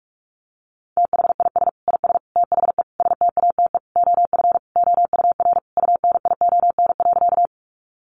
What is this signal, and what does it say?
Morse code, heard as a keyed tone, about 44 wpm, 700 Hz. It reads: THIS IS THE STATE OF OUR UNION.
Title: Sentences from Top 1400 Words 45wpm